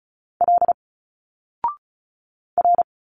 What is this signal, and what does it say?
Morse code, heard as a keyed tone, about 36 wpm, 700 Hz. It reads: L R
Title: Single Letters 35wpm